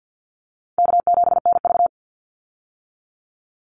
Morse code, heard as a keyed tone, about 50 wpm, 700 Hz. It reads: K7NV
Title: Call Signs - 50wpm